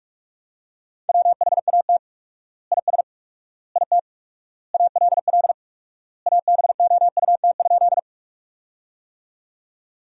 Morse code, heard as a keyed tone, about 47 wpm, 700 Hz. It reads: WHAT IS IT ALL ABOUT?
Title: 4x Word-Spacing - Sentences from Top 100 Words - 45wpm